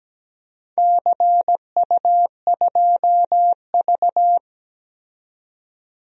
Morse code, heard as a keyed tone, about 17 wpm, 700 Hz. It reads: CU2V
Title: DXCC-100 Call Signs - 4 Characters - 17wpm